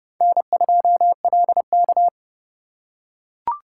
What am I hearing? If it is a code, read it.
Morse code, heard as a keyed tone, about 30 wpm, 700 Hz. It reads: N2LK
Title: K1USN SST QSO Practice - 30wpm